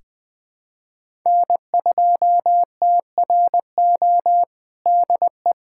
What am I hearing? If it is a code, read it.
Morse code, heard as a keyed tone, about 21 wpm, 700 Hz. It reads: N2TRO DE
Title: USA Callsign DE Callsign - 5 Characters - 20wpm